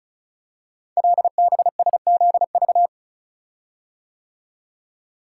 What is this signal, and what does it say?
Morse code, heard as a keyed tone, about 36 wpm, 700 Hz. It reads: LBSZV
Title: Code Groups - 5 Letters - 35wpm